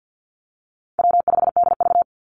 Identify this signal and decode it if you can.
Morse code, heard as a keyed tone, about 50 wpm, 700 Hz. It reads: W5DV
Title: Call Signs Repeated 3x - 50wpm